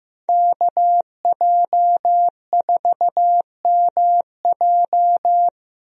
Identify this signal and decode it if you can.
Morse code, heard as a keyed tone, about 15 wpm, 700 Hz. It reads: KJ4MJ
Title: USA Call Signs - 5 Characters - Repeated 2x - 15wpm